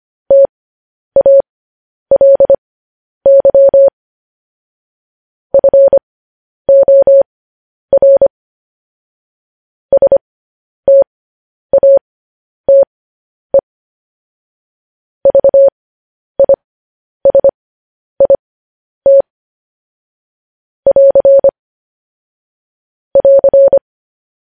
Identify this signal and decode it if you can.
Morse code, heard as a keyed tone, about 25 wpm, spaced as for 10 wpm, 550 Hz. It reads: TALY FOR STATE VISIT <AR> <AR>
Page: news headlines in morse code at 10 wpm